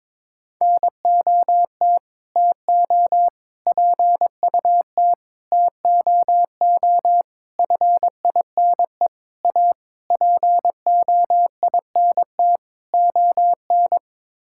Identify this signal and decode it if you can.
Morse code, heard as a keyed tone, about 21 wpm, 700 Hz. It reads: NOT TO PUT TOO FINE A POINT ON
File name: practice / 22wpm / Sets of 8 Words - 22wpm